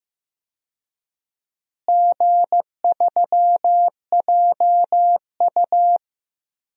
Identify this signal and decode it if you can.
Morse code, heard as a keyed tone, about 15 wpm, 700 Hz. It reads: G3JU
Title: DXCC-100 Call Signs - 4 Characters - Repeated 2x - 15wpm